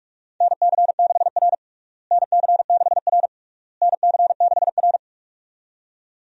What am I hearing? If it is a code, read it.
Morse code, heard as a keyed tone, about 45 wpm, 700 Hz. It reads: NC6R NC6R NC6R
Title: USA Call Signs - 4 Characters - Repeated 3x - 45wpm